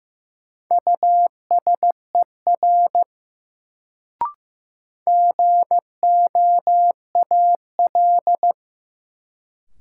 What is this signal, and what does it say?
Morse code, heard as a keyed tone, about 15 wpm, 700 Hz. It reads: USER GOAL
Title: Sentences from Top 1000 Words - Review Part 5 - New Word Drill - 15wpm